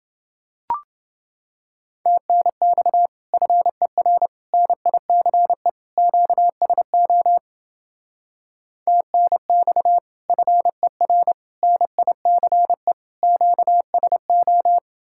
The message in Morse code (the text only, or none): TNX FER NICE QSO TNX FER NICE QSO